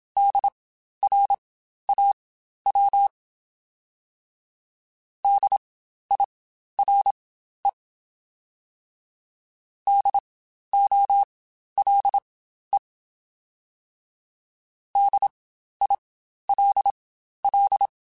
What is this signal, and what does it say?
Morse code, heard as a keyed tone, about 27 wpm, spaced as for 12 wpm, 800 Hz. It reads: DRAW DIRE DOLE DILL
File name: Spaced Alphas / Spaced Alphas mp3s / Ds sp @25